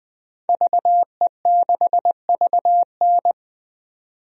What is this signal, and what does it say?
Morse code, heard as a keyed tone, about 20 wpm, 700 Hz. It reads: VE6VN
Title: DXCC-100 Call Signs - 5 Characters - 20wpm